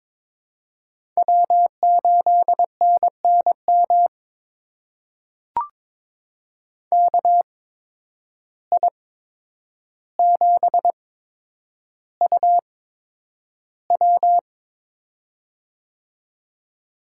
Morse code, heard as a keyed tone, about 22 wpm, spaced as for 9 wpm, 700 Hz. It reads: W8NNM KI7UW
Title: USA Call Signs - 5 Characters - 8x Character Spacing - 22wpm